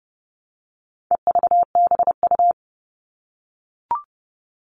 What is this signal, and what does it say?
Morse code, heard as a keyed tone, about 30 wpm, 700 Hz. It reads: EVBU
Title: All Cut Numbers - 4-Digit 30wpm